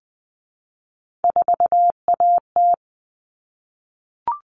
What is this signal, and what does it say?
Morse code, heard as a keyed tone, about 20 wpm, 700 Hz. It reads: 4AT
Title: All Cut Numbers - 3-Digit 20wpm